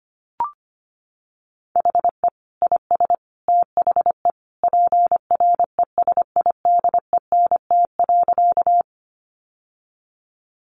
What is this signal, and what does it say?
Morse code, heard as a keyed tone, about 25 wpm, 700 Hz. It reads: HE IS THE PRESIDENT.
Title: Sentences from Top 1000 Words 25wpm